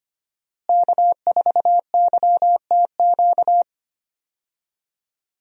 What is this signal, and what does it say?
Morse code, heard as a keyed tone, about 25 wpm, 700 Hz. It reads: K4YTQ
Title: USA Call Signs - 5 Characters - Repeated 3x - 25wpm